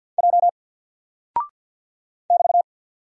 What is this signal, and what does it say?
Morse code, heard as a keyed tone, about 48 wpm, 700 Hz. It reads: J <BT>
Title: Single Character (Letter:Number:Punctuation:Prosign) Alt 50wpm